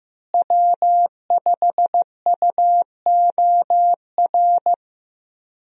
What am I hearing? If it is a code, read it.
Morse code, heard as a keyed tone, about 15 wpm, 700 Hz. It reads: W5UOR
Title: USA Call Signs - 5 Characters - Repeated 3x - 15wpm